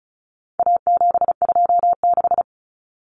Morse code, heard as a keyed tone, about 34 wpm, 700 Hz. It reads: A726
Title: Common Cut Numbers - 4-Digit 35wpm